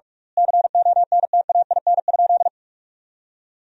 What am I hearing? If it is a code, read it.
Morse code, heard as a keyed tone, about 46 wpm, 700 Hz. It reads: CONTAIN?
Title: 2x Word-Spacing - Sentences from Top 500 Words - 45wpm